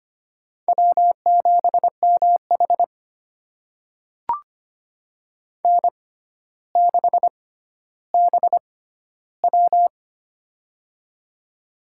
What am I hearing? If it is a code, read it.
Morse code, heard as a keyed tone, about 25 wpm, spaced as for 13 wpm, 700 Hz. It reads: W7MH N6BW
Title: USA Call Signs - 6x Character Spacing - 25wpm